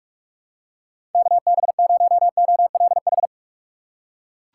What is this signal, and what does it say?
Morse code, heard as a keyed tone, about 46 wpm, 700 Hz. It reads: KB0OLH
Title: USA Call Signs - 6 Characters - 3x Character Spacing - 45wpm